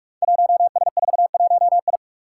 Morse code, heard as a keyed tone, about 46 wpm, 700 Hz. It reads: 1S41I
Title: Five Challenging Letter-Number Combinations 45wpm